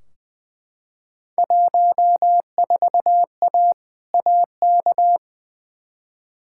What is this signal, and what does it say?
Morse code, heard as a keyed tone, about 20 wpm, 700 Hz. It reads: 14A AK
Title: Field Day Exchanges Encore - 20wpm